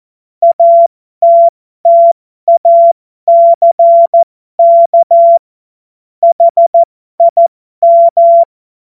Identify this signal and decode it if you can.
Morse code, heard as a keyed tone, about 14 wpm, spaced as for 12 wpm, 680 Hz. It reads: ATTACK HIM
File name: Morsekurs / Witze